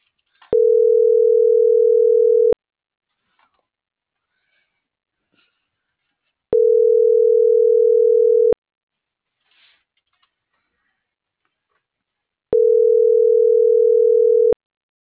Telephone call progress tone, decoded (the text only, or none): ringback tone